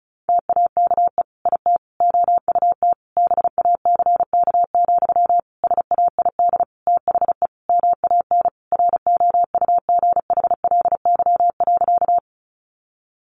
Morse code, heard as a keyed tone, about 34 wpm, 700 Hz. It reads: TAKE IT OUT BACK, SAID THE MAN ROUGHLY.
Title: Sentences from Top 1500 Words 35wpm